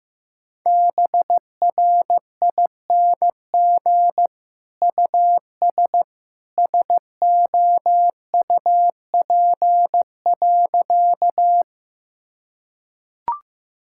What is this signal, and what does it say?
Morse code, heard as a keyed tone, about 16 wpm, 700 Hz. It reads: BRING US SOUP.
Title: Sentences from Top 1000 Words - Review Part 6 - 15wpm